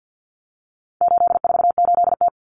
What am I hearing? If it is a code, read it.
Morse code, heard as a keyed tone, about 51 wpm, 700 Hz. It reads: 848T
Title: Common Cut Numbers - 4-Digit 50wpm